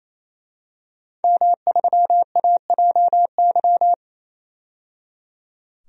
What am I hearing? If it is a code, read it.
Morse code, heard as a keyed tone, about 28 wpm, 700 Hz. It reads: M3AJY
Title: DXCC-100 Call Signs - 5 Characters - 28wpm